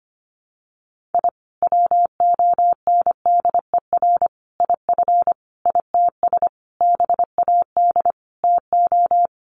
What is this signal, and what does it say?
Morse code, heard as a keyed tone, about 25 wpm, 700 Hz. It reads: I WONDER IF ITS BAD TO
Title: Sentences from Top 700 Words 25wpm